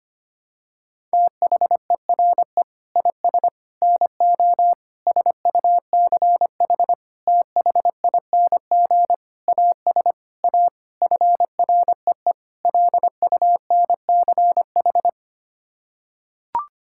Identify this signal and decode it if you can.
Morse code, heard as a keyed tone, about 25 wpm, 700 Hz. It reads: THERE IS NO SUCH THING AS A FREE LUNCH
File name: Sets of 9 Words - 25wpm